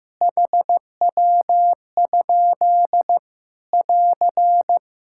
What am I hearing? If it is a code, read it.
Morse code, heard as a keyed tone, about 15 wpm, 700 Hz. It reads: HW? <AR>
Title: Practice QSO - Exchange 1 - Beginning - 15wpm